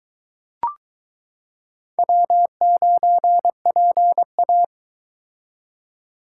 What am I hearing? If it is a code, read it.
Morse code, heard as a keyed tone, about 22 wpm, 700 Hz. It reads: W9PA